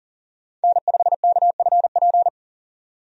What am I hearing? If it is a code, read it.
Morse code, heard as a keyed tone, about 40 wpm, 700 Hz. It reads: N5KFP
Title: USA Call Signs - 5 Characters - 40wpm